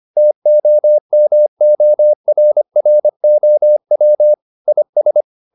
Morse code, heard as a keyed tone, about 25 wpm, 600 Hz. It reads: TOMORROW IS
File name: the old man and the sea part 1 25wpm